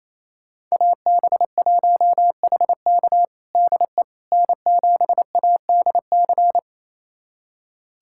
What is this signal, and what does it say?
Morse code, heard as a keyed tone, about 28 wpm, 700 Hz. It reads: AB1HK DE N7ADC
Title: USA Callsign DE Callsign - 5 Characters - 28wpm